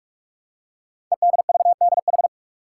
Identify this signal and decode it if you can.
Morse code, heard as a keyed tone, about 44 wpm, 700 Hz. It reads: EDVDH
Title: Code Groups - 5 Letters - 45wpm